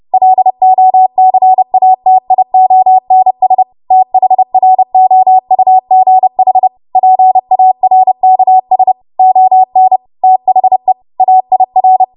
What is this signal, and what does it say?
Morse code, heard as a keyed tone, about 30 wpm, 750 Hz. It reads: LOCATIONS THROUGH PARKS ON THE AIR
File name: Morse / 30WPM